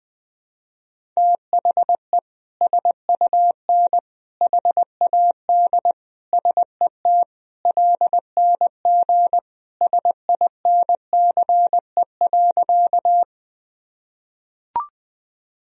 Morse code, heard as a keyed tone, about 21 wpm, 700 Hz. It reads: THE SUN HAD SET LNG SINCE.
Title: Sentences from Top 300 Words and Abbreviations 20wpm